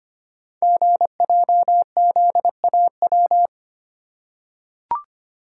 Morse code, heard as a keyed tone, about 25 wpm, 700 Hz. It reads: GJZAW
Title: Code Groups - 5 Letters - 25wpm